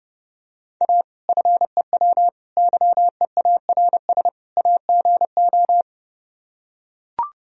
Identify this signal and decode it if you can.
Morse code, heard as a keyed tone, about 30 wpm, 700 Hz. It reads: A FEW YEARS AGO